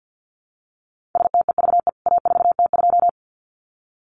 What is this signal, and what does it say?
Morse code, heard as a keyed tone, about 48 wpm, 700 Hz. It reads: STEVE A4T2
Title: CWT Exchange 50wpm